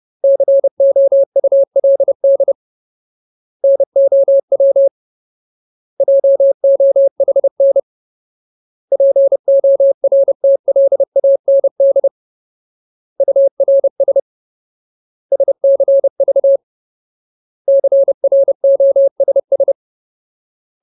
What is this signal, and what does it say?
Morse code, heard as a keyed tone, about 30 wpm, 550 Hz. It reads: COULD NOW JOHN PORTLAND URS SCV CROSS
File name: Ultimate Word List / 30WPM Ultimate